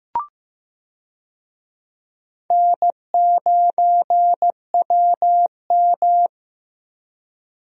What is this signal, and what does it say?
Morse code, heard as a keyed tone, about 15 wpm, 700 Hz. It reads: N9WM
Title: USA Call Signs - 4 Characters - Repeated 2x - 15wpm